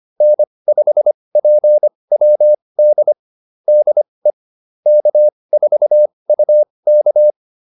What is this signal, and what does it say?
Morse code, heard as a keyed tone, about 25 wpm, spaced as for 20 wpm, 600 Hz. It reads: N5PWD DE K4UK